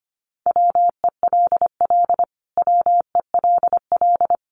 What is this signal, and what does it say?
Morse code, heard as a keyed tone, about 25 wpm, 700 Hz. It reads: WELL WELL
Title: Top 200 Words Repeated 25wpm